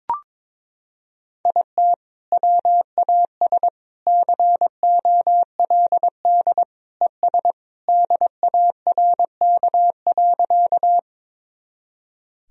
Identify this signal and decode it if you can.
Morse code, heard as a keyed tone, about 22 wpm, 700 Hz. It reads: IT WAS COLD ES DARK.
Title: Sentences from Top 500 Words and Abbreviations 22wpm